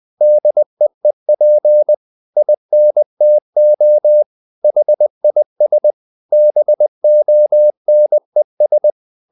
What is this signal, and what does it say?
Morse code, heard as a keyed tone, about 20 wpm, 600 Hz. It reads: DEEP INTO HIS BONES
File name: the old man and the sea part 2 20wpm